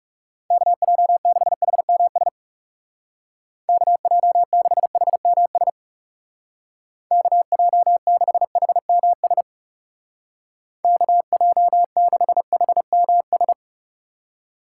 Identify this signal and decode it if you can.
Morse code, heard as a keyed tone, about 35 wpm, 700 Hz. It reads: KJ6HMS KJ6HMS KJ6HMS KJ6HMS
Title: Speed Racing - USA 2x3 Call Signs 30wpm